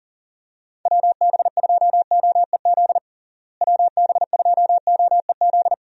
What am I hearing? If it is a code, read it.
Morse code, heard as a keyed tone, about 40 wpm, 700 Hz. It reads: WB2OEZ WB2OEZ
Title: USA Call Signs - 6 Characters - Repeated 2x - 40wpm